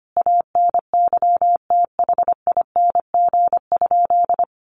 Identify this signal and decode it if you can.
Morse code, heard as a keyed tone, about 25 wpm, 700 Hz. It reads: ANYTHING?
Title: Sentences from Top 1000 Words 25wpm